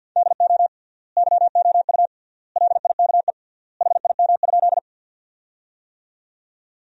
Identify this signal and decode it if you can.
Morse code, heard as a keyed tone, about 51 wpm, 700 Hz. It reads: DO YOU LIKE HIM?
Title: 3x Word-Spacing - Sentences from Top 100 Words - 50wpm